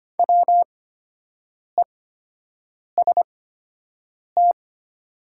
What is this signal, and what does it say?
Morse code, heard as a keyed tone, about 25 wpm, spaced as for 7 wpm, 700 Hz. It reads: WEST
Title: 8x Character Spacing - Top 500 Words - 25wpm